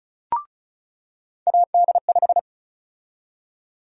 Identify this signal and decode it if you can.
Morse code, heard as a keyed tone, about 36 wpm, 700 Hz. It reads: AD5
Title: USA Call Sign Prefixes - 35wpm